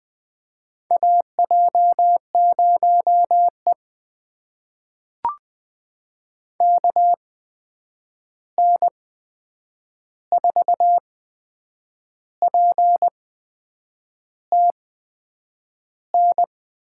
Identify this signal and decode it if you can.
Morse code, heard as a keyed tone, about 20 wpm, spaced as for 8 wpm, 700 Hz. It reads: AJ0E KN4PTN
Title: USA Call Signs - 8x Character Spacing - 20wpm